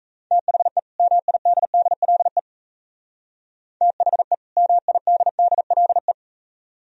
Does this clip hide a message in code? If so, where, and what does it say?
Yes: Morse code from 0.3 to 6.1 s, heard as a keyed tone, about 36 wpm, 690 Hz. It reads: THE MIDDLE THE MIDDLE